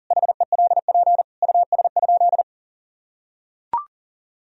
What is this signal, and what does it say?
Morse code, heard as a keyed tone, about 37 wpm, 700 Hz. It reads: HELP US?
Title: Sentences from Top 1000 Words - Review Part 3 - 40wpm